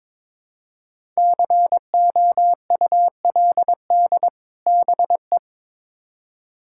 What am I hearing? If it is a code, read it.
Morse code, heard as a keyed tone, about 22 wpm, 700 Hz. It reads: COULD BE
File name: Build A Phrase - 2-Word Phrases - 22wpm